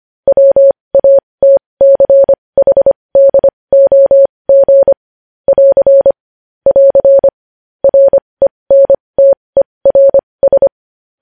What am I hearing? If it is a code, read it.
Morse code, heard as a keyed tone, about 25 wpm, spaced as for 20 wpm, 550 Hz. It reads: WATCHDOG <AR> <AR> RENTERS
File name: podcast / News headlines-1x-25x20